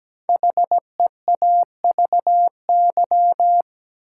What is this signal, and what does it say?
Morse code, heard as a keyed tone, about 17 wpm, 700 Hz. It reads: HEAVY
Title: Sentences from Top 1000 Words - Review Part 6 - New Word Drill - 17wpm